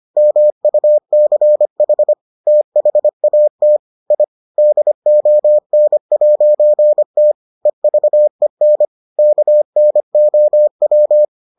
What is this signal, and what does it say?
Morse code, heard as a keyed tone, about 25 wpm, 600 Hz. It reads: MUCH THAT I DON'T EVEN KNOW